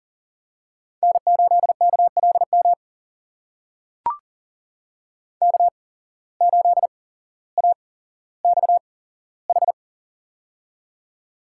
Morse code, heard as a keyed tone, about 40 wpm, spaced as for 17 wpm, 700 Hz. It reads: N8KLM K8AXH
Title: USA Call Signs - 5 Characters - 8x Character Spacing - 40wpm